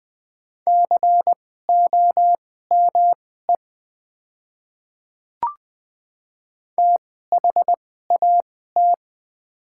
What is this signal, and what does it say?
Morse code, heard as a keyed tone, about 20 wpm, spaced as for 14 wpm, 700 Hz. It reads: COME THAT
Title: 2x Character Spacing - Top 300 Words - 20wpm